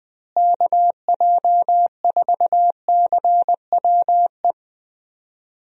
Morse code, heard as a keyed tone, about 20 wpm, 700 Hz. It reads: KJ4CWE
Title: USA Call Signs - 6 Characters - Repeated 2x - 20wpm